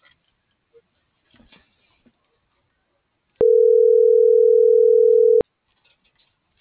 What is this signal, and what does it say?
Ringback tone (North America), 440 + 480 Hz.